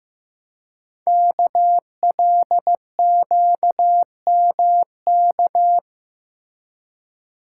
Morse code, heard as a keyed tone, about 15 wpm, 700 Hz. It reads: KLQMK
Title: Code Groups - 5 Letters - 15wpm